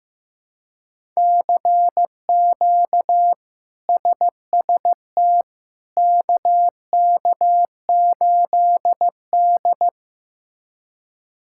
Morse code, heard as a keyed tone, about 16 wpm, 700 Hz. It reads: CQ SST KK8D
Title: K1USN SST QSO Practice - 15wpm